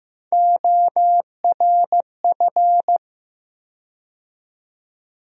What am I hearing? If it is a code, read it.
Morse code, heard as a keyed tone, about 15 wpm, 700 Hz. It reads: ORF
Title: USA Call Sign Postfixes - 3 Letters 15wpm